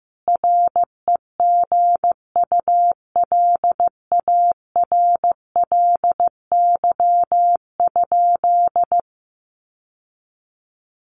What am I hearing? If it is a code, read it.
Morse code, heard as a keyed tone, about 15 wpm, 700 Hz. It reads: REGULARLY?
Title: Sentences from Top 1400 Words 15wpm